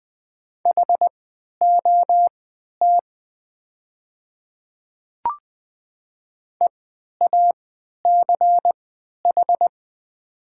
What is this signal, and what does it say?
Morse code, heard as a keyed tone, about 20 wpm, spaced as for 11 wpm, 700 Hz. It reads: HOT EACH
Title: 3x Character Spacing - Top 100 Words - 20wpm